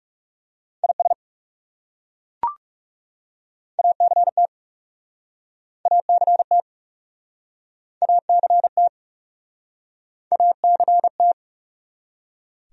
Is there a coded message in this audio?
Yes: Morse code from 0.8 to 11.3 s, heard as a keyed tone, about 41 wpm, 700 Hz. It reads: IS ACT ACT ACT ACT